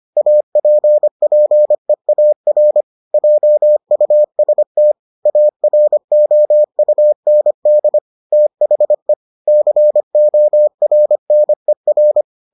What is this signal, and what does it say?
Morse code, heard as a keyed tone, about 25 wpm, 600 Hz. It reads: APPEAR JUST AROUND THE CORNER